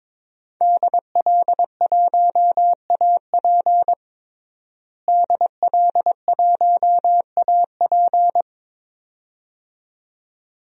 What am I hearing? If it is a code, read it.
Morse code, heard as a keyed tone, about 22 wpm, 700 Hz. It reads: DL1AP DL1AP
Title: DXCC-100 Call Signs - 5 Characters - Repeated 2x - 22wpm